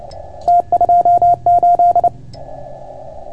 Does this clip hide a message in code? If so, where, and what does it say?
Yes: Morse code from 0.5 to 2.1 s, heard as a keyed tone, about 28 wpm, 690 Hz. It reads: T28